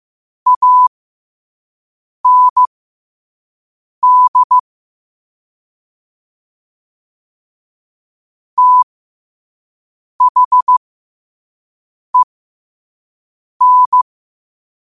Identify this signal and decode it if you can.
Morse code, heard as a keyed tone, about 15 wpm, spaced as for 5 wpm, 1000 Hz. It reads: AND THEN
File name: qotdfiles / 5wpm